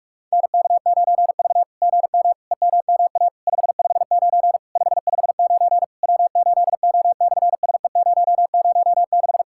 Morse code, heard as a keyed tone, about 43 wpm, 700 Hz. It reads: NK9V GM EMMA 559 559 W8O/SE006
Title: SOTA QSO Practice 45wpm